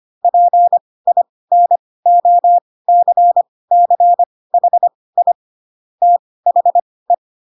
Morse code, heard as a keyed tone, about 25 wpm, spaced as for 18 wpm, 700 Hz. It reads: PINOCCHI THE
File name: Pinocchio ch3 12 25 ik1hge